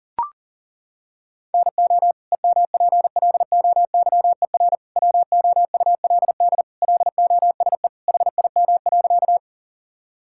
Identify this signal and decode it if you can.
Morse code, heard as a keyed tone, about 40 wpm, 700 Hz. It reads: NO EMPLOYER WOULD LOSE HIM.
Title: Sentences from Top 1600 Words 40wpm